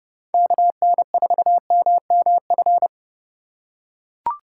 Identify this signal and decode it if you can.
Morse code, heard as a keyed tone, about 28 wpm, 700 Hz. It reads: KN4MMF